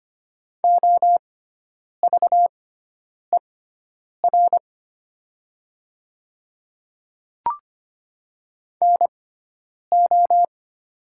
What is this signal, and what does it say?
Morse code, heard as a keyed tone, about 25 wpm, spaced as for 9 wpm, 700 Hz. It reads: OVER NO
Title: 6x Character Spacing - Top 200 Words - 25wpm